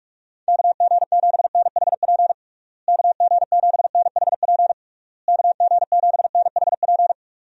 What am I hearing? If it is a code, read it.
Morse code, heard as a keyed tone, about 46 wpm, 700 Hz. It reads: KG7NHP KG7NHP KG7NHP
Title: USA Call Signs - 6 Characters - Repeated 3x - 45wpm